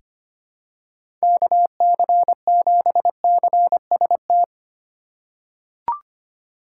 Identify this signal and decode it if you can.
Morse code, heard as a keyed tone, about 25 wpm, 700 Hz. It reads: KC7CST
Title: USA Call Signs - 6 Characters - 3x Character Spacing - 25wpm